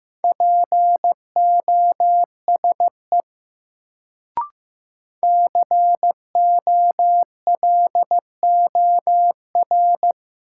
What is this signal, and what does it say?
Morse code, heard as a keyed tone, about 15 wpm, 700 Hz. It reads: POSE COLOR
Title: Sentences from Top 1000 Words - Review Part 2 - New Word Drill - 15wpm